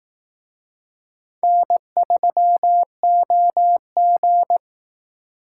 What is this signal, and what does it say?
Morse code, heard as a keyed tone, about 18 wpm, 700 Hz. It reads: N3OG
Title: Speed Racing - USA 1x2 Call Signs 15wpm